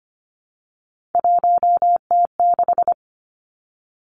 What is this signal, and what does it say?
Morse code, heard as a keyed tone, about 25 wpm, 700 Hz. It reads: 1T6
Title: Three Challenging Letter-Number Combinations 25wpm